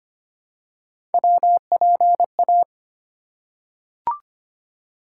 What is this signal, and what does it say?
Morse code, heard as a keyed tone, about 25 wpm, 700 Hz. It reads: WPA
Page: morse-code-ninja-practice-25wpm